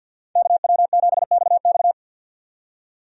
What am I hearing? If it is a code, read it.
Morse code, heard as a keyed tone, about 50 wpm, 700 Hz. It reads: KW7XX
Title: USA Call Signs - 5 Characters - 50wpm